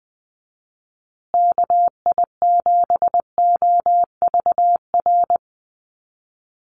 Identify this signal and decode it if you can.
Morse code, heard as a keyed tone, about 20 wpm, 700 Hz. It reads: KI7OVR
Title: USA Call Signs Repeated 3x 20wpm